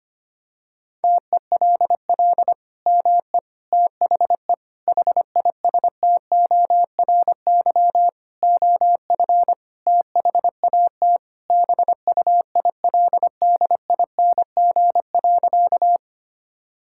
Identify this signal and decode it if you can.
Morse code, heard as a keyed tone, about 25 wpm, 700 Hz. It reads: TELL ME THE HISTORY OF THAT BUILDING.
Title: Sentences from Top 1000 Words - Review Part 5 - 25wpm